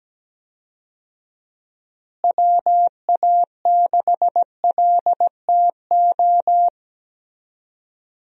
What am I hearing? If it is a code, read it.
Morse code, heard as a keyed tone, about 17 wpm, 700 Hz. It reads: WA6LTO
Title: USA Call Signs - 6 Characters - Repeated 2x - 17wpm